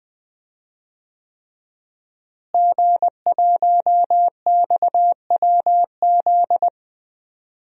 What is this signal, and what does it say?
Morse code, heard as a keyed tone, about 20 wpm, 700 Hz. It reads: G1XWZ
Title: DXCC-100 Call Signs - 5 Characters - Repeated 3x - 20wpm